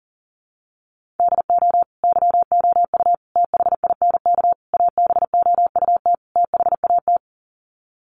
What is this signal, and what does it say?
Morse code, heard as a keyed tone, about 40 wpm, 700 Hz. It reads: DO YOU THINK ABOUT THAT
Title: Sets of 5 Words Encore2 40wpm